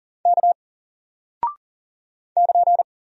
Morse code, heard as a keyed tone, about 40 wpm, 700 Hz. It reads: K <KN>
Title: Single Character (Letter:Number:Punctuation:Prosign) Alt 40wpm